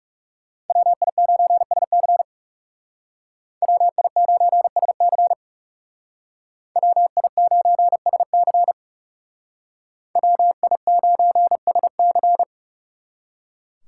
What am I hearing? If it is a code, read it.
Morse code, heard as a keyed tone, about 38 wpm, 700 Hz. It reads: WI9SC WI9SC WI9SC WI9SC